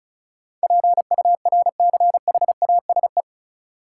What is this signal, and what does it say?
Morse code, heard as a keyed tone, about 36 wpm, 700 Hz. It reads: PURCHASE
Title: Sentences from Top 800 Words - New Word Drill - 35wpm